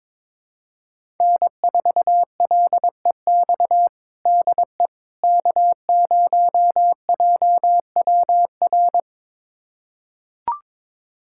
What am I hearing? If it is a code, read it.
Morse code, heard as a keyed tone, about 21 wpm, 700 Hz. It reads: N4LEX DE K0JWR
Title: USA Callsign DE Callsign - 5 Characters - 22wpm